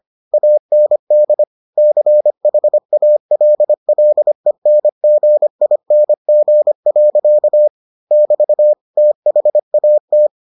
Morse code, heard as a keyed tone, about 25 wpm, 600 Hz. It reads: AND CHALLENGING. <BT> THAT